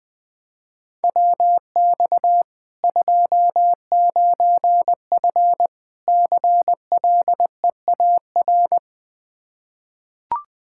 Morse code, heard as a keyed tone, about 21 wpm, 700 Hz. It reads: WX 29F CLEAR
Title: Practice QSO - Exchange 2 - Info - 20wpm